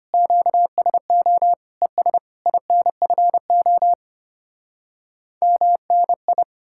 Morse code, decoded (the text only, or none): QSO ES INFO MNI